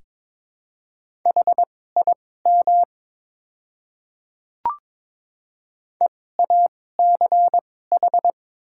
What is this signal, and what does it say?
Morse code, heard as a keyed tone, about 22 wpm, spaced as for 16 wpm, 700 Hz. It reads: HIM EACH
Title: 2x Character Spacing - Top 300 Words - 22wpm